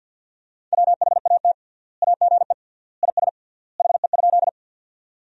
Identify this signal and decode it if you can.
Morse code, heard as a keyed tone, about 53 wpm, 700 Hz. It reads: WHAT AGE IS HE?
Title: 3x Word-Spacing - Sentences from Top 500 Words - 50wpm